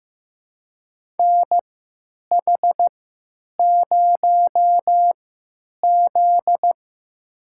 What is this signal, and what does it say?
Morse code, heard as a keyed tone, about 15 wpm, spaced as for 9 wpm, 700 Hz. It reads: NH0Z
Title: USA Call Signs - 4 Characters - 3x Character Spacing - 15wpm